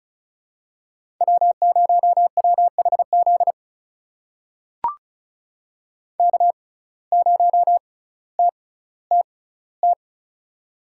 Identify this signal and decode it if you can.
Morse code, heard as a keyed tone, about 36 wpm, spaced as for 18 wpm, 700 Hz. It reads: W0WHZ K0TTT
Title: USA Call Signs - 5 Characters - 6x Character Spacing - 35wpm